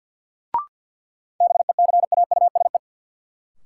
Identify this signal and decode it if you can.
Morse code, heard as a keyed tone, about 52 wpm, 700 Hz. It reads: BECAUSE
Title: Sentences from Top 600 Words - New Word Drill - 50wpm